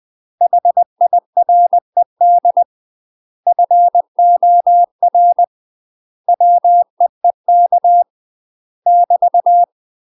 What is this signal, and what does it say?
Morse code, heard as a keyed tone, about 20 wpm, 700 Hz. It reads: HIRED FOR WEEK <BT>
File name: UK MASTER QSOs TRAINING / 20WPM / UK MASTER QSOs 0025